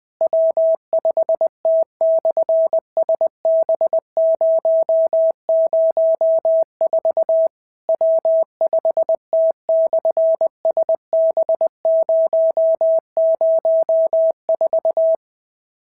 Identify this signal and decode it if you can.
Morse code, heard as a keyed tone, about 20 wpm, 650 Hz. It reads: W5T/SB004 W5T/SB004
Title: SOTA QSO Practice 20wpm